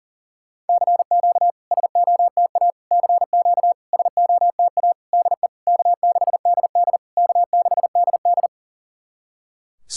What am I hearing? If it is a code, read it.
Morse code, heard as a keyed tone, about 40 wpm, 690 Hz. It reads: CQ SOTA CQ SOTA DE K6DD K6DD
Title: SOTA QSO Practice 40wpm